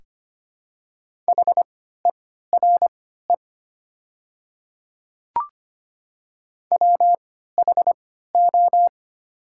Morse code, heard as a keyed tone, about 25 wpm, spaced as for 14 wpm, 700 Hz. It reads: HERE WHO
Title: 3x Character Spacing - Top 200 Words - 25wpm